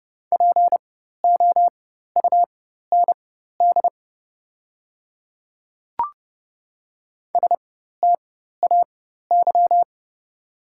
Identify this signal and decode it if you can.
Morse code, heard as a keyed tone, about 30 wpm, spaced as for 14 wpm, 700 Hz. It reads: POUND STAY
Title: 4x Character Spacing - Top 500 Words - 30wpm